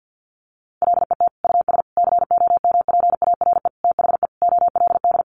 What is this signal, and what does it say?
Morse code, heard as a keyed tone, about 50 wpm, 700 Hz. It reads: LET US COMPARE THE OLD
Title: Sentences from Top 1300 Words 50wpm